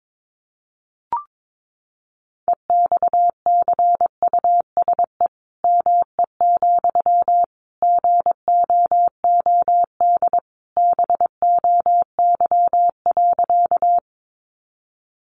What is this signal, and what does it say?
Morse code, heard as a keyed tone, about 22 wpm, 700 Hz. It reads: EXCUSE ME, GOOD BOY.
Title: Sentences from Top 700 Words 22wpm